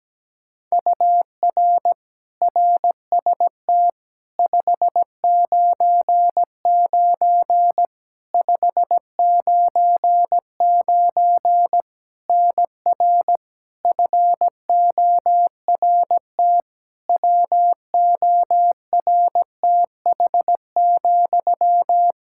Morse code, heard as a keyed tone, about 17 wpm, 700 Hz. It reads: UR RST 599 599 NR FORT WORTH,
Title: Practice QSO - Exchange 1 - Beginning - 17wpm